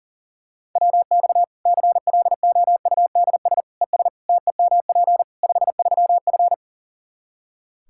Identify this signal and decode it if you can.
Morse code, heard as a keyed tone, about 37 wpm, 700 Hz. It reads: WX CLOUDS ES TEMP 53F
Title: Practice QSO - Exchange 2 - Info - 40wpm